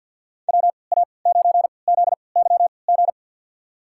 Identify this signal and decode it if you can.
Morse code, heard as a keyed tone, about 51 wpm, spaced as for 28 wpm, 700 Hz. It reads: WA9ZYG
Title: USA Call Signs - 6 Characters - 3x Character Spacing - 50wpm